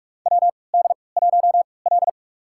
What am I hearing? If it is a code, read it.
Morse code, heard as a keyed tone, about 45 wpm, spaced as for 25 wpm, 690 Hz. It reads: WD1L
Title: USA Call Signs - 4 Characters - 3x Character Spacing - 45wpm